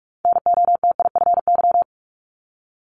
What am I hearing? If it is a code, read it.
Morse code, heard as a keyed tone, about 46 wpm, 690 Hz. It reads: NOTIFY
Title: Multi-letter Phonemes Repeated 3x 45wpm